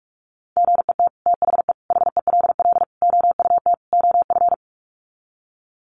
Morse code, heard as a keyed tone, about 44 wpm, 700 Hz. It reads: GET THE HELL OUT OF
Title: Sets of 5 Words Encore2 45wpm